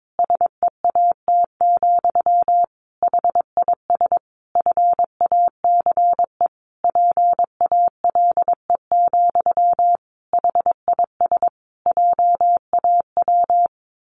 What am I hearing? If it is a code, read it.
Morse code, heard as a keyed tone, about 22 wpm, 700 Hz. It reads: SEAT, HIS FACE PALE, HIS JAW
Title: A Sound Of Thunder 22wpm